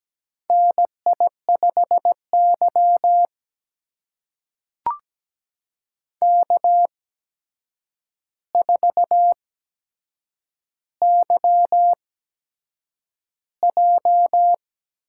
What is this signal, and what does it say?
Morse code, heard as a keyed tone, about 17 wpm, spaced as for 7 wpm, 700 Hz. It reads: NI5Y K4YJ
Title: USA Call Signs - 4 Characters - 8x Character Spacing - 17wpm